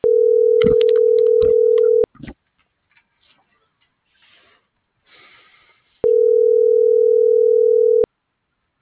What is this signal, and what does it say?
Ringback tone (North America), 440 + 480 Hz.